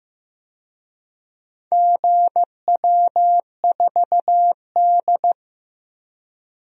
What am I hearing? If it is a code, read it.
Morse code, heard as a keyed tone, about 16 wpm, 700 Hz. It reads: GW4D
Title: DXCC-100 Call Signs - 4 Characters - 15wpm